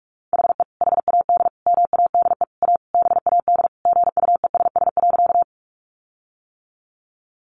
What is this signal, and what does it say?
Morse code, heard as a keyed tone, about 44 wpm, 700 Hz. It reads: HE HAD MADE A BAD GUESS.
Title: Sentences from Top 1500 Words 45wpm